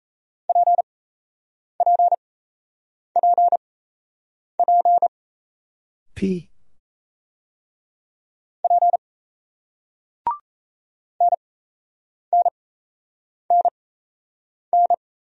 Morse code, heard as a keyed tone, about 38 wpm, spaced as for 8 wpm, 700 Hz. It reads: PPPP P NNNN